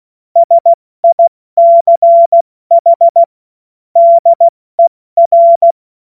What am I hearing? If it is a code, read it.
Morse code, heard as a keyed tone, about 16 wpm, 680 Hz. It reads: SICH DER
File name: Morsekurs / Witze / Witz 16. August 70 F80